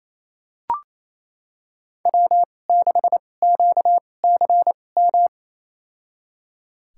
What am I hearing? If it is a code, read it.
Morse code, heard as a keyed tone, about 27 wpm, spaced as for 20 wpm, 700 Hz. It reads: W6QCM
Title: USA Call Signs - 5 Characters - 2x Character Spacing - 28wpm